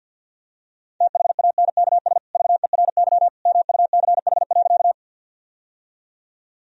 Morse code, heard as a keyed tone, about 50 wpm, 700 Hz. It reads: THANKS VERY MUCH.
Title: Sentences from Top 1000 Words - Review Part 3 - 50wpm